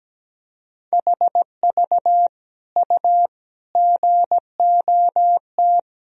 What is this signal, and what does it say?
Morse code, heard as a keyed tone, about 17 wpm, 700 Hz. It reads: HV U GOT
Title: Sentences from Top 300 Words and Abbreviations 17wpm